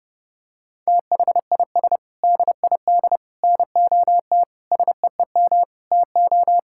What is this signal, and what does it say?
Morse code, heard as a keyed tone, about 30 wpm, 700 Hz. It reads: THIS DID NOT SEEM TO
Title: Sentences from Top 1000 Words - Review Part 6 - 30wpm